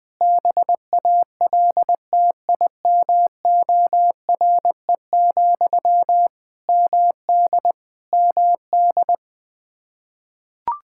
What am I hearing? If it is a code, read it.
Morse code, heard as a keyed tone, about 20 wpm, 700 Hz. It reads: BALTIMORE, MD MD
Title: Practice QSO - Exchange 1 - Beginning - 20wpm